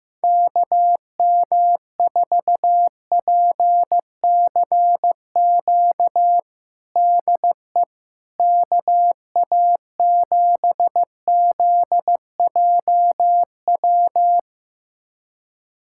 Morse code, heard as a keyed tone, about 15 wpm, 700 Hz. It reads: KM4PCQ DE KA7ZJW
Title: USA Callsign DE Callsign - 6 Characters - 15wpm